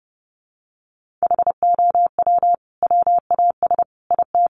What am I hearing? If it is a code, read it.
Morse code, heard as a keyed tone, about 30 wpm, 700 Hz. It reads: HOW WAS IT
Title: Sentences from Top 500 Words 30wpm